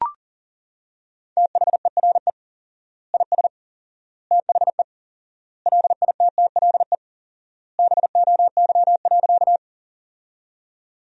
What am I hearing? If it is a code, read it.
Morse code, heard as a keyed tone, about 40 wpm, 700 Hz. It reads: THERE IS THE LITTLE BOY.
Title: 4x Word-Spacing - Sentences from Top 200 Words - 40wpm